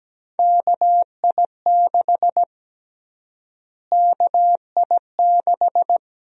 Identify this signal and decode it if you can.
Morse code, heard as a keyed tone, about 17 wpm, 700 Hz. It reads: KI6 KI6
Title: USA Call Sign Prefixes - Repeated 2x - 17wpm